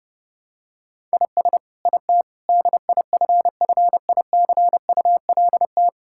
Morse code, heard as a keyed tone, about 32 wpm, 700 Hz. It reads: IS IT DIFFICULT
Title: Sentences from Top 1000 Words - Review Part 5 - 30wpm